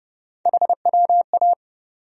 Morse code, heard as a keyed tone, about 30 wpm, 700 Hz. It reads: HWA
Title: USA Call Sign Postfixes - 3 Letters 30wpm